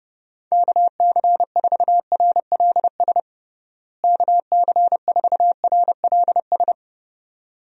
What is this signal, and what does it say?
Morse code, heard as a keyed tone, about 30 wpm, 700 Hz. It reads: KC4RLS KC4RLS
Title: USA Call Signs - 6 Characters - Repeated 2x - 30wpm